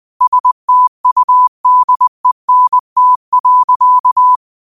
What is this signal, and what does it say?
Morse code, heard as a keyed tone, about 20 wpm, 1000 Hz. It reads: STUDENT.
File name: qotdfiles / 20wpm